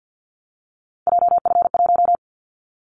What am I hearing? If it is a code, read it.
Morse code, heard as a keyed tone, about 50 wpm, 700 Hz. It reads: JF1
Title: Callsign Prefixes for Top 25 DXCC - Repeated 3x - 50wpm